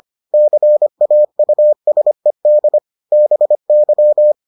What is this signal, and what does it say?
Morse code, heard as a keyed tone, about 25 wpm, 600 Hz. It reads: CAUSED BY